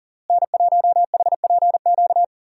Morse code, heard as a keyed tone, about 39 wpm, 700 Hz. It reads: N1HPQ
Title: USA Call Signs - 5 Characters - Repeated 2x - 40wpm